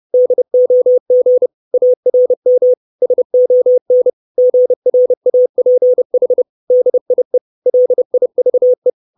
Morse code, heard as a keyed tone, about 30 wpm, 500 Hz. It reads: DOG ARM SON GRAPH DIE LIVE
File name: 632 words / 632 Words 30WPM